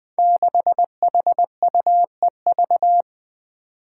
Morse code, heard as a keyed tone, about 20 wpm, 700 Hz. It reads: 6HUEV
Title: Five Challenging Letter-Number Combinations 20wpm